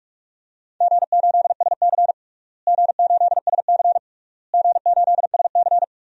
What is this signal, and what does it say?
Morse code, heard as a keyed tone, about 42 wpm, 700 Hz. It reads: G8SC G8SC G8SC
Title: DXCC-100 Call Signs - 4 Characters - Repeated 3x - 45wpm